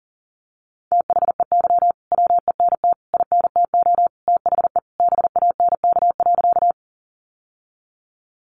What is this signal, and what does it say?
Morse code, heard as a keyed tone, about 40 wpm, 700 Hz. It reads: THEY WENT INTO THE BANK.
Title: Sentences from Top 900 Words 40wpm